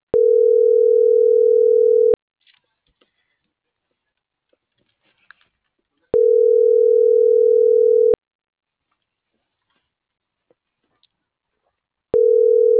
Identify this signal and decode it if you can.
Ringback tone (North America), 440 + 480 Hz.